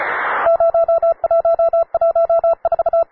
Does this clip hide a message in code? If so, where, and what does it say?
Yes: Morse code from 0.5 to 3.0 s, heard as a keyed tone, about 34 wpm, 670 Hz. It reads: 0114